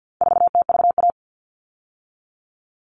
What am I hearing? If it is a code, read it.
Morse code, heard as a keyed tone, about 50 wpm, 700 Hz. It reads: 4TVA
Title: All Cut Numbers - 4-Digit 50wpm